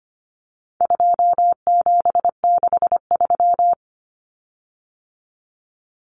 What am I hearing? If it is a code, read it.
Morse code, heard as a keyed tone, about 25 wpm, 700 Hz. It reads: 2763
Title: Common Cut Numbers - 4-Digit 25wpm